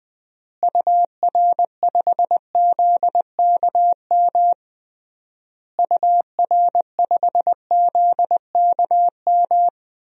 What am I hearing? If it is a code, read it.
Morse code, heard as a keyed tone, about 20 wpm, 700 Hz. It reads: UR5ZKM UR5ZKM
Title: DXCC-100 Call Signs - 6 Characters - Repeated 3x - 20wpm